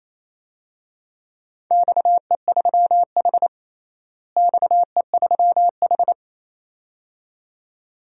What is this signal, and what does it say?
Morse code, heard as a keyed tone, about 28 wpm, 700 Hz. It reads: XE3H XE3H
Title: DXCC-100 Call Signs - 4 Characters - Repeated 2x - 28wpm